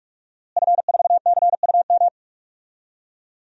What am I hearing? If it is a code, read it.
Morse code, heard as a keyed tone, about 46 wpm, 700 Hz. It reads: F4CUM
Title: DXCC-100 Call Signs - 5 Characters - Repeated 2x - 45wpm